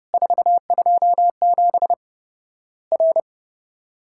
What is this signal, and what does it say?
Morse code, heard as a keyed tone, about 30 wpm, 700 Hz. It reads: 427 R
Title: Worked All Europe DX Contest - QTC Exchange - 30wpm